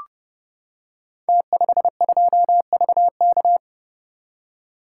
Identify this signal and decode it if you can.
Morse code, heard as a keyed tone, about 30 wpm, 700 Hz. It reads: T52VK
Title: Code Groups - 5 Letter:Numbers - 30wpm